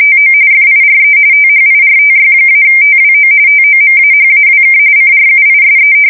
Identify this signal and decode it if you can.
RTTY, about 45.45 baud. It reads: TO FIRE THIRTY-TWO TOMAHAWK CRUISE
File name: fsk170 / battleshipnj